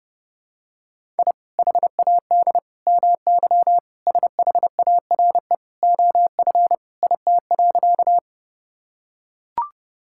Morse code, heard as a keyed tone, about 30 wpm, 700 Hz. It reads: I HAD MY SHARE OF IT.
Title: Sentences from Top 1000 Words - Review Part 6 - 30wpm